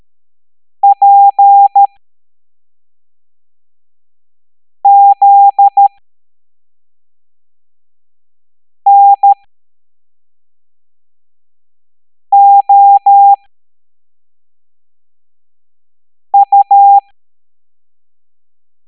Morse code, heard as a keyed tone, about 13 wpm, spaced as for 3 wpm, 800 Hz. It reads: PZNOU
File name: hamdata / L09-EX04